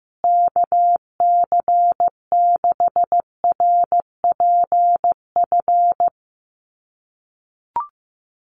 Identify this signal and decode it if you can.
Morse code, heard as a keyed tone, about 15 wpm, 700 Hz. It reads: KC6RPF
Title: USA Call Signs Repeated 3x 15wpm